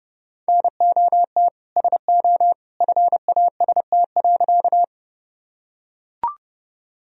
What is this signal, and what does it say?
Morse code, heard as a keyed tone, about 32 wpm, 700 Hz. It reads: NOT SO FAST.
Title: Sentences from Top 1000 Words - Review Part 2 - 30wpm